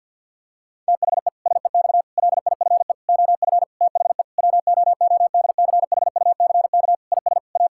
Morse code, heard as a keyed tone, about 50 wpm, 700 Hz. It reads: THE SEX LIFE OF THE WOODCHUCK IS A
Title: Pangram Challenge v2 - 50wpm